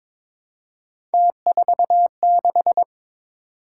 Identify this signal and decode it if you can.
Morse code, heard as a keyed tone, about 21 wpm, 700 Hz. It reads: T46
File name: practice / 22wpm / Three Letter-Number Combinations - 22wpm